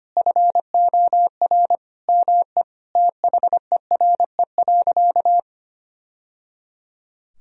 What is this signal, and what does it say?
Morse code, heard as a keyed tone, about 25 wpm, 700 Hz. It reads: FOR ME THERE.
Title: Sentences from Top 1000 Words - Review Part 4 - 25wpm